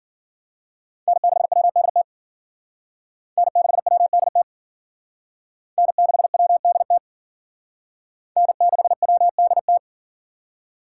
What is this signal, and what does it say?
Morse code, heard as a keyed tone, about 50 wpm, 700 Hz. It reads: N6WDT N6WDT N6WDT N6WDT
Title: Speed Racing - USA 1x3 Call Signs 40wpm